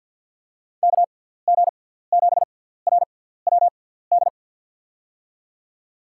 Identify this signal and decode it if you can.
Morse code, heard as a keyed tone, about 52 wpm, spaced as for 17 wpm, 700 Hz. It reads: KG7RWD
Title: USA Call Signs - 6 Characters - 6x Character Spacing - 50wpm